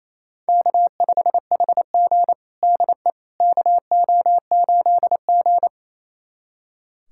Morse code, heard as a keyed tone, about 28 wpm, 700 Hz. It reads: K5HG DE KO8G
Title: USA Callsign DE Callsign - 4 Characters - 28wpm